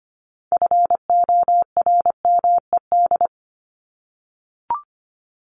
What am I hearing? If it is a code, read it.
Morse code, heard as a keyed tone, about 25 wpm, 700 Hz. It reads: FORMED
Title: ED Suffix - Top 100 Words with Base Words 25wpm